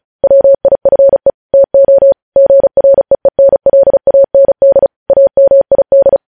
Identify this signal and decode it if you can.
Morse code, heard as a keyed tone, about 36 wpm, 550 Hz. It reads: WIFE TO GREENLAND AMID
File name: podcast / News headlines-1x-35x35